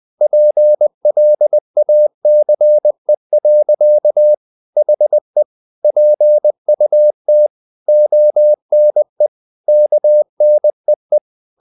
Morse code, heard as a keyed tone, about 20 wpm, 600 Hz. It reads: PLACE. HE PUT ONE KNEE
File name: the old man and the sea / the old man and the sea part 1 20wpm